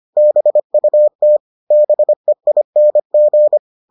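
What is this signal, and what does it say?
Morse code, heard as a keyed tone, about 25 wpm, 600 Hz. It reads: BUT BEING
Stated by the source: Morse edition of the Foundations of Amateur Radio episodes.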